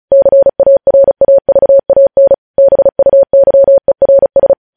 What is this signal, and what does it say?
Morse code, heard as a keyed tone, about 36 wpm, 550 Hz. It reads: CARAVAN BUYERS
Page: news headlines in morse code at 35 wpm